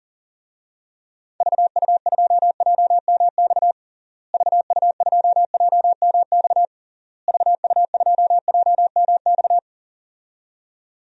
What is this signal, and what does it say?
Morse code, heard as a keyed tone, about 40 wpm, 700 Hz. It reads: VU2JMX VU2JMX VU2JMX
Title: DXCC-100 Call Signs - 6 Characters - Repeated 3x - 40wpm